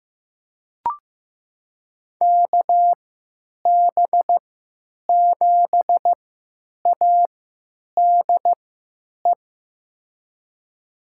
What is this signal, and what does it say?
Morse code, heard as a keyed tone, about 15 wpm, spaced as for 9 wpm, 700 Hz. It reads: KB7ADE
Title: USA Call Signs - 6 Characters - 3x Character Spacing - 15wpm